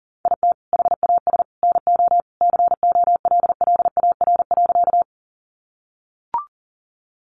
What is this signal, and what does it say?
Morse code, heard as a keyed tone, about 40 wpm, 700 Hz. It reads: IT HAS NO COLLAR.
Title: Easy Phrases 40wpm